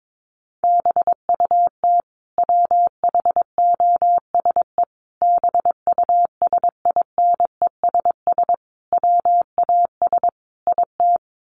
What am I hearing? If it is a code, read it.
Morse code, heard as a keyed tone, about 22 wpm, 700 Hz. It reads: BUT WHOSE BUSINESS WAS IT
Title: Sentences from Top 1500 Words 22wpm